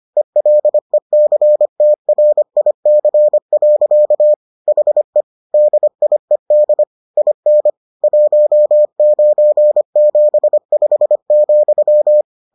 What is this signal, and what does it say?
Morse code, heard as a keyed tone, about 25 wpm, 600 Hz. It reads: ELECTRIC. HE DIED IN 1975,